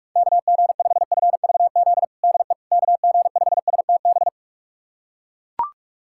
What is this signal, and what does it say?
Morse code, heard as a keyed tone, about 41 wpm, 690 Hz. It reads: KG5FVZ DE KG5STB
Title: USA Callsign DE Callsign - 6 Characters - 45wpm